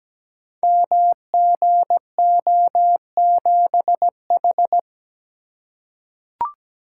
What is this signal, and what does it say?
Morse code, heard as a keyed tone, about 17 wpm, 700 Hz. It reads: MGO7H
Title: Code Groups - 5 Letter:Numbers - 17wpm